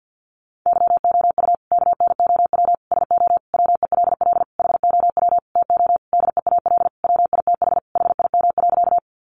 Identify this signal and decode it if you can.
Morse code, heard as a keyed tone, about 50 wpm, 700 Hz. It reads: YOU KNOW SO WELL HOW TO DEAL WITH HIM.
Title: Sentences from Top 800 Words 50wpm